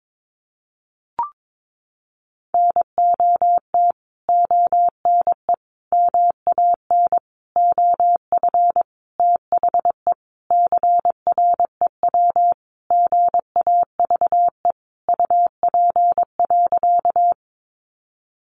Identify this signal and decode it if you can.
Morse code, heard as a keyed tone, about 22 wpm, 700 Hz. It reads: NOT ONE MAN OF THE CREW GAVE UP.
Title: Sentences from Top 1500 Words 22wpm